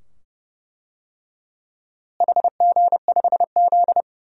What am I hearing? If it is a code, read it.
Morse code, heard as a keyed tone, about 32 wpm, 700 Hz. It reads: HG5Z
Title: DXCC-100 Call Signs - 4 Characters - Repeated 2x - 30wpm